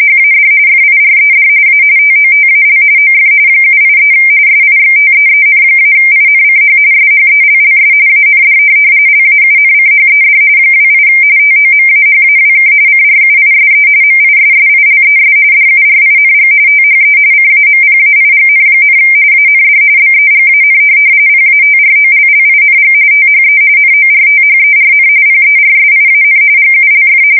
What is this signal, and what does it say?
RTTY, about 45.45 baud. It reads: HAD FIRED A TOTAL OF 783 SIXTEEN-INCH SHELLS AND LAUNCHED 28 TOMAHAWK CRUISE MISSILES DURING THE CAMPAIGN, AND BEGAN PATROLLING THE NORTHERN PERSIAN GULF
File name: fsk170 / USS-Missouri